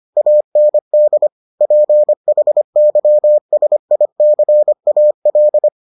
Morse code, heard as a keyed tone, about 25 wpm, 600 Hz. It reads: AND PHYSICAL